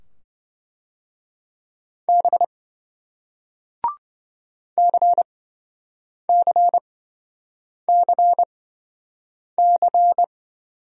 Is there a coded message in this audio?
Yes: Morse code from 2.1 to 10.2 s, heard as a keyed tone, about 27 wpm, spaced as for 7 wpm, 700 Hz. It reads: B CCCC